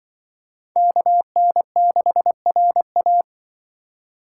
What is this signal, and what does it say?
Morse code, heard as a keyed tone, about 25 wpm, 700 Hz. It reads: KN6RA
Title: Speed Racing - USA 2x2 Call Signs 20wpm